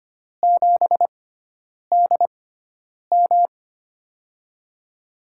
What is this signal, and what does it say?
Morse code, heard as a keyed tone, about 25 wpm, spaced as for 9 wpm, 700 Hz. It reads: 7DM
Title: USA Call Signs - 4 Characters - 6x Character Spacing - 25wpm